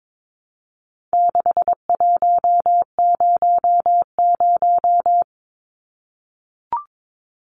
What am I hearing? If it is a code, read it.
Morse code, heard as a keyed tone, about 22 wpm, 700 Hz. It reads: 6100
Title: Numbers - 4-Digit 22wpm